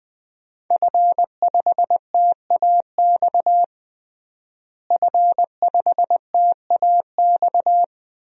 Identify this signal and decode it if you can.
Morse code, heard as a keyed tone, about 20 wpm, 700 Hz. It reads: F5TAX F5TAX
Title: DXCC-100 Call Signs - 5 Characters - Repeated 2x - 20wpm